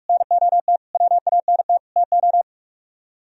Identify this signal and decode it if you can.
Morse code, heard as a keyed tone, about 48 wpm, 700 Hz. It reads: NOT WANT TO